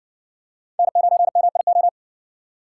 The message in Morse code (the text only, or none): N9MIO